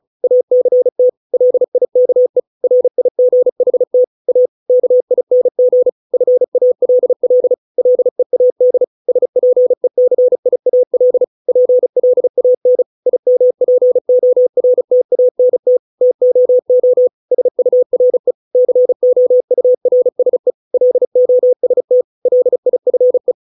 Morse code, heard as a keyed tone, about 34 wpm, 500 Hz. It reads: ACT LIKE RIGHT A KING FALL LEAD SPECIAL PLAN IMPORTANT TOO SURE COURSE LOST LIFE